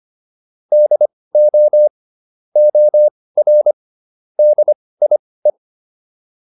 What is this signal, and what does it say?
Morse code, heard as a keyed tone, about 25 wpm, spaced as for 18 wpm, 600 Hz. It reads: DO OR DIE